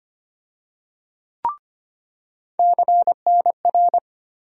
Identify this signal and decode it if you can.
Morse code, heard as a keyed tone, about 25 wpm, 700 Hz. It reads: CNR
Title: USA Call Sign Postfixes - 3 Letters 25wpm